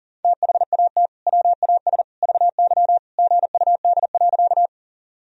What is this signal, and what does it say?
Morse code, heard as a keyed tone, about 40 wpm, 700 Hz. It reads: THAT WAS VY GUD.
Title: Sentences from Top 200 Words and Abbreviations 40wpm